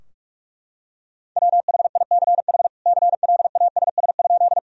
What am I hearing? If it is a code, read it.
Morse code, heard as a keyed tone, about 41 wpm, 700 Hz. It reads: WHICH CLASS?
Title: Sentences from Top 1000 Words - Review Part 2 - 45wpm